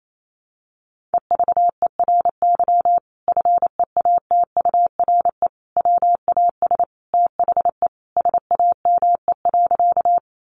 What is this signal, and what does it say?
Morse code, heard as a keyed tone, about 28 wpm, 700 Hz. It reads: EVERY FEATURE WAS THE SAME.
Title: Sentences from Top 800 Words 28wpm